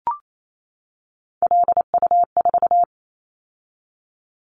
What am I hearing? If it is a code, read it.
Morse code, heard as a keyed tone, about 28 wpm, 700 Hz. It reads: LU4
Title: Callsign Prefixes for Top 50 DXCC - 28wpm